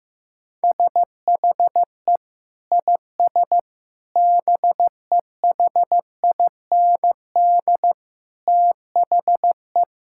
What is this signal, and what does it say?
Morse code, heard as a keyed tone, about 15 wpm, 700 Hz. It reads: SHE IS BEHIND THE
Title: Sentences from Top 1000 Words - Review Part 6 - 15wpm